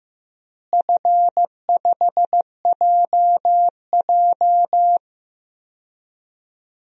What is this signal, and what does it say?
Morse code, heard as a keyed tone, about 15 wpm, 700 Hz. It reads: F5JJ
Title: DXCC-100 Call Signs - 4 Characters - Repeated 3x - 15wpm